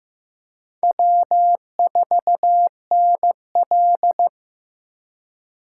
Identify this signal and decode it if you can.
Morse code, heard as a keyed tone, about 15 wpm, 700 Hz. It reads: W4NL
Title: USA Call Signs - 4 Characters - 4x Character Spacing - 15wpm